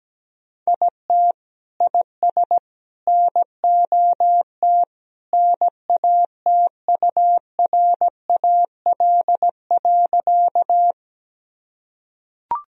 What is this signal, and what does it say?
Morse code, heard as a keyed tone, about 17 wpm, 700 Hz. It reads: IT IS NOT NATURAL.
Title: Sentences from Top 1000 Words - Review Part 4 - 17wpm